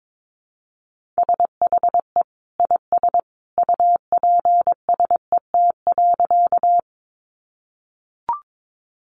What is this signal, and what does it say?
Morse code, heard as a keyed tone, about 22 wpm, 700 Hz. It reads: SHE IS UPSET.
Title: Sentences from Top 800 Words 22wpm